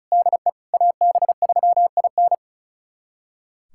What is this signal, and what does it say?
Morse code, heard as a keyed tone, about 34 wpm, 690 Hz. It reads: DE AB3IN
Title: USA Callsign DE Callsign - 5 Characters - 35wpm